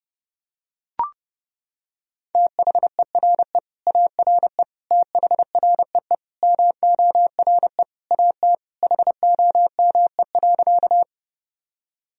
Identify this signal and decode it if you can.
Morse code, heard as a keyed tone, about 30 wpm, 700 Hz. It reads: THERE ARE THREE MORE AT HOME.
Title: Sentences from Top 1000 Words - Review Part 1 - 30wpm